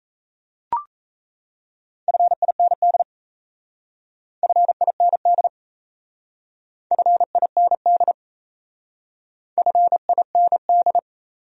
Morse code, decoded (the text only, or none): FIND FIND FIND FIND